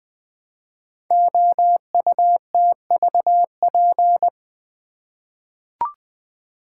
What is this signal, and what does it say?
Morse code, heard as a keyed tone, about 20 wpm, 700 Hz. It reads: OUTVP
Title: Code Groups - 5 Letter:Numbers - 20wpm